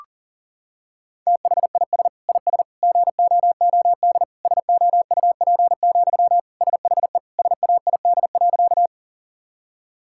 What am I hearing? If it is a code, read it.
Morse code, heard as a keyed tone, about 37 wpm, 700 Hz. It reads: THIS IS GOOD SOUP, SHE SAID.
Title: Sentences from Top 1000 Words - Review Part 6 - 40wpm